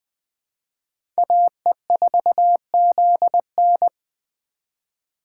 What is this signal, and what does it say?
Morse code, heard as a keyed tone, about 20 wpm, 700 Hz. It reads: AE4ZN
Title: Speed Racing - USA 2x2 Call Signs 17wpm